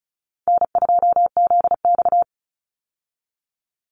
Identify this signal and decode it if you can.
Morse code, heard as a keyed tone, about 34 wpm, 700 Hz. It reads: N2ZX
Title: Easy Call Signs Repeated 2x - 35wpm